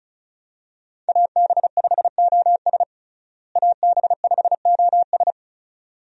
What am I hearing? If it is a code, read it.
Morse code, heard as a keyed tone, about 34 wpm, 700 Hz. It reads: AB5OS AB5OS
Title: USA Call Signs - 5 Characters - Repeated 2x - 35wpm